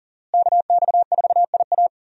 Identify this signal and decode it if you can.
Morse code, heard as a keyed tone, about 40 wpm, 700 Hz. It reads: KX4IA
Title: USA Call Signs - 5 Characters - Repeated 3x - 40wpm